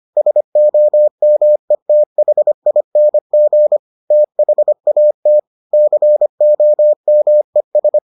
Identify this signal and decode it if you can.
Morse code, heard as a keyed tone, about 25 wpm, 600 Hz. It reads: SOMETHING THAT COMES